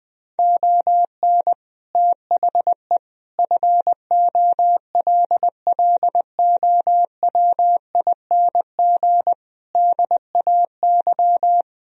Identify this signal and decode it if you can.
Morse code, heard as a keyed tone, about 20 wpm, 700 Hz. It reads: ON THE FOLLOWING DAY
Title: Sentences from Top 1000 Words - Review Part 4 - 20wpm